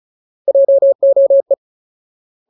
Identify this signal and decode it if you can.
Morse code, heard as a keyed tone, about 34 wpm, 550 Hz. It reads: JOE
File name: Ultimate Word List / 35WPM Ultimate